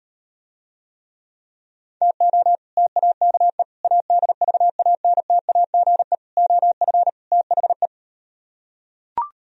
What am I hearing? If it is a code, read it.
Morse code, heard as a keyed tone, about 38 wpm, 700 Hz. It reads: TO TAKE ADVANTAGE OF THE